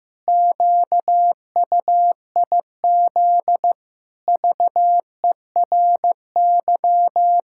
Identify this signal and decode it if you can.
Morse code, heard as a keyed tone, about 16 wpm, 700 Hz. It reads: QUIZ VERY
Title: Pangram Challenge v2 - 15wpm